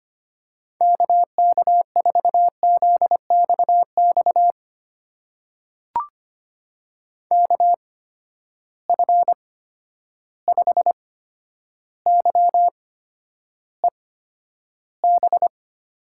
Morse code, heard as a keyed tone, about 25 wpm, spaced as for 11 wpm, 700 Hz. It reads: KK4ZXX KF5YEB
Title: USA Call Signs - 6 Characters - 8x Character Spacing - 25wpm